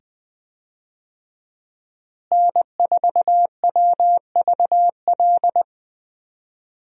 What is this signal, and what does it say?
Morse code, heard as a keyed tone, about 20 wpm, 700 Hz. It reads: N4WVL
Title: USA Call Signs - 5 Characters - Repeated 2x - 20wpm